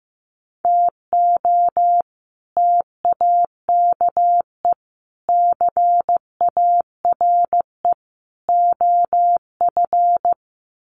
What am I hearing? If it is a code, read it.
Morse code, heard as a keyed tone, about 15 wpm, 700 Hz. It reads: TO TAKE CARE OF
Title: Sets of 4 Words 15wpm